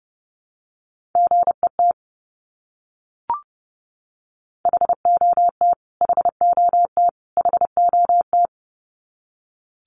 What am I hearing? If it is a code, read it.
Morse code, heard as a keyed tone, about 30 wpm, 700 Hz. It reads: GET HOT HOT HOT
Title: Top 200 Words Repeated 30wpm